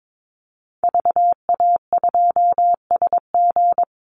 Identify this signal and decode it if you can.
Morse code, heard as a keyed tone, about 22 wpm, 700 Hz. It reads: VA2SG
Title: Call Signs Repeated 2x - 22wpm